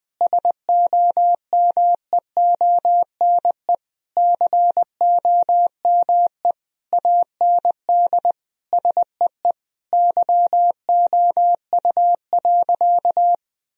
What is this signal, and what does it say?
Morse code, heard as a keyed tone, about 20 wpm, 700 Hz. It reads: SOMEONE COME AND SEE YOU.
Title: Sentences from Top 1000 Words - Review Part 4 - 20wpm